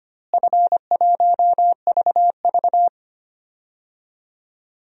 Morse code, heard as a keyed tone, about 25 wpm, 700 Hz. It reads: F1VV
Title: DXCC-100 Call Signs - 4 Characters - Repeated 3x - 25wpm